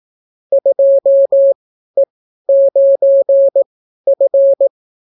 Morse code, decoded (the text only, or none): 2E9F